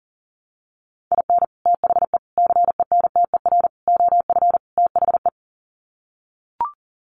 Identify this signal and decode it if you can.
Morse code, heard as a keyed tone, about 40 wpm, 700 Hz. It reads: IN THE CENTER OF THE
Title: Sets of 5 Words 40wpm